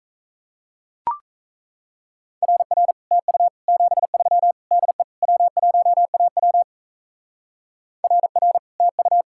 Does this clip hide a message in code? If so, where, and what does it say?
Yes: Morse code from 2.4 to 9.2 s, heard as a keyed tone, about 35 wpm, 700 Hz. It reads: RR TU 73 DE W1AW RR TU